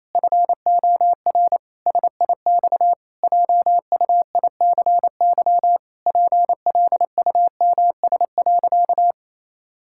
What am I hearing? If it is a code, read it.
Morse code, heard as a keyed tone, about 28 wpm, 700 Hz. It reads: FOR SIX JUICY PLUMS.
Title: Pangram Challenge v2 - 28wpm